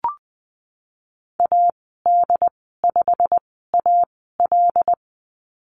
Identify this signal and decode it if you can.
Morse code, heard as a keyed tone, about 20 wpm, spaced as for 15 wpm, 700 Hz. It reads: AD5AL
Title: 088 Lesson 15g - Number 5 - Call Signs 20wpm